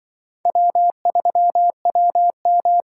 Morse code, heard as a keyed tone, about 25 wpm, 700 Hz. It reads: W3WM